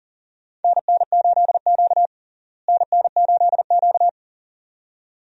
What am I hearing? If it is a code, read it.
Morse code, heard as a keyed tone, about 37 wpm, 700 Hz. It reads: NN8Q NN8Q
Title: USA Call Signs - 4 Characters - Repeated 2x - 40wpm